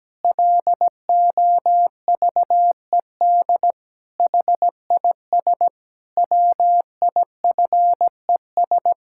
Morse code, heard as a keyed tone, about 17 wpm, 700 Hz. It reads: LOVED HIS WIFES
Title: Pangram Challenge v2 - 17wpm